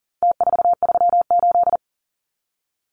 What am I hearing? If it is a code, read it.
Morse code, heard as a keyed tone, about 40 wpm, 700 Hz. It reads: T438
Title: Numbers - Contest Style - 4-Digit 40wpm